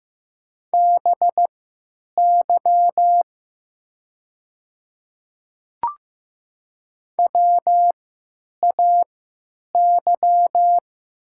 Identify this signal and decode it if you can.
Morse code, heard as a keyed tone, about 15 wpm, spaced as for 9 wpm, 700 Hz. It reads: BY WAY
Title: 3x Character Spacing - Top 200 Words - 15wpm